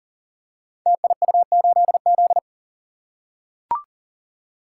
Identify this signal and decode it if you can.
Morse code, heard as a keyed tone, about 42 wpm, 700 Hz. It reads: TIU8Z
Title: Code Groups - 5 Letter:Numbers - 40wpm